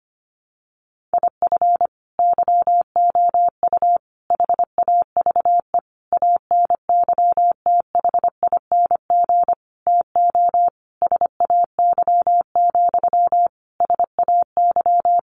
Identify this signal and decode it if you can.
Morse code, heard as a keyed tone, about 25 wpm, 700 Hz. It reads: IF YOU HAVE ANYTHING TO SAY, SAY
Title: Sentences from Top 1000 Words 25wpm